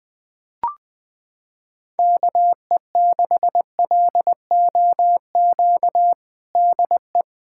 Morse code, heard as a keyed tone, about 21 wpm, 700 Hz. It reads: KE6LOQ DE
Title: USA Callsign DE Callsign 20wpm